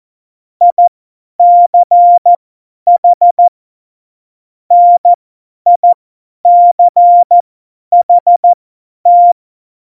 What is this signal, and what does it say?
Morse code, heard as a keyed tone, about 14 wpm, spaced as for 10 wpm, 700 Hz. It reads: ICH NICHT
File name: Morsekurs / Witze / Witz 21 April 2020, 50 F70